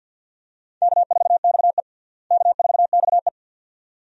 Morse code, heard as a keyed tone, about 51 wpm, 700 Hz. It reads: K4XE K4XE
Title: USA Call Signs - 4 Characters - Repeated 2x - 50wpm